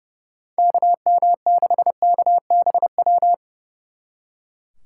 Morse code, heard as a keyed tone, about 37 wpm, 700 Hz. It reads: KM6KBW
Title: Speed Racing - USA Call Signs 30wpm